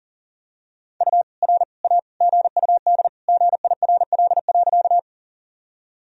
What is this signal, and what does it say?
Morse code, heard as a keyed tone, about 40 wpm, 700 Hz. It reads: U R A GUD GIRL.
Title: Sentences from Top 400 Words and Abbreviations 40wpm